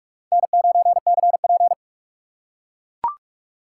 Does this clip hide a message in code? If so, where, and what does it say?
Yes: Morse code from 0.3 to 1.7 s, heard as a keyed tone, about 44 wpm, 700 Hz. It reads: N9CP